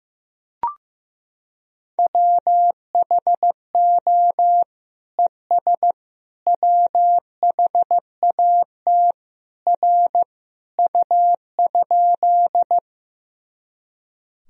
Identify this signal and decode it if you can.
Morse code, heard as a keyed tone, about 15 wpm, 700 Hz. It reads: WHO ES WHAT R U?
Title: Sentences from Top 100 Words and Abbreviations 15wpm